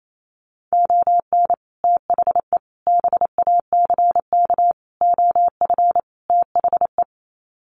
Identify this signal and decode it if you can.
Morse code, heard as a keyed tone, about 28 wpm, 700 Hz. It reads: ON THE BACK OF THE
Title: Sets of 5 Words 28wpm